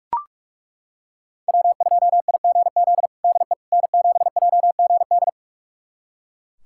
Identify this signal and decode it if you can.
Morse code, heard as a keyed tone, about 42 wpm, 700 Hz. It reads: W2IGZ DE N7JGD
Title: USA Callsign DE Callsign - 5 Characters - 45wpm